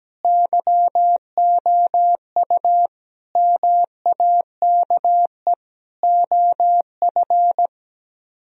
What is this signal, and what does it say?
Morse code, heard as a keyed tone, about 17 wpm, 700 Hz. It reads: YOU MAKE OF
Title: Speed-Racing - Sets of 5 Words 17wpm